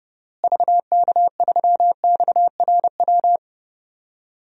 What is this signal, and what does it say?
Morse code, heard as a keyed tone, about 31 wpm, 700 Hz. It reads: VK3XRW
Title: DXCC-100 Call Signs - 6 Characters - Repeated 2x - 30wpm